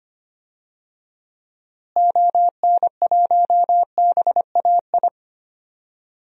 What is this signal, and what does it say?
Morse code, heard as a keyed tone, about 25 wpm, 700 Hz. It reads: ON1BAI
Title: DXCC-100 Call Signs - 6 Characters - Repeated 2x - 25wpm